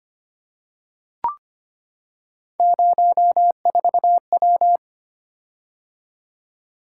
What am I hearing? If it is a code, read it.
Morse code, heard as a keyed tone, about 25 wpm, 700 Hz. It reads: 04W
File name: Three Letter-Number Combinations - 25wpm